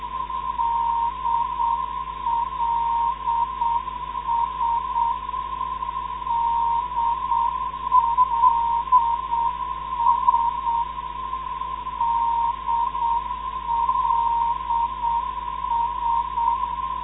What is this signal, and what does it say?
Morse code, heard as a keyed tone, about 7 wpm, 940 Hz. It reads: DLS DLS DLS
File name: ndb zvok / se 412khz